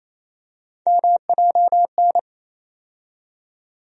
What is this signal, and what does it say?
Morse code, heard as a keyed tone, about 27 wpm, 700 Hz. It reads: MJN
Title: USA Call Sign Postfixes - 3 Letters 28wpm